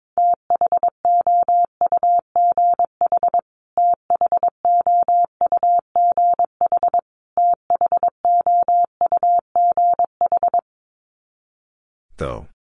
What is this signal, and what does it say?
Morse code, heard as a keyed tone, about 22 wpm, 700 Hz. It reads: THOUGH THOUGH THOUGH
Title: Top 500 Words Repeated 22wpm